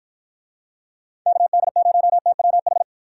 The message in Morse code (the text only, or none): KD0TWH